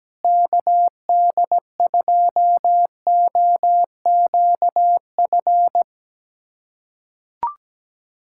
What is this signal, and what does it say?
Morse code, heard as a keyed tone, about 17 wpm, 700 Hz. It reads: KD2OQF
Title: USA Callsign DE Callsign - 6 Characters - 17wpm